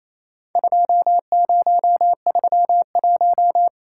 Morse code, heard as a keyed tone, about 28 wpm, 700 Hz. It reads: 2031
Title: Worked All Europe DX Contest - QTC Exchange - 28wpm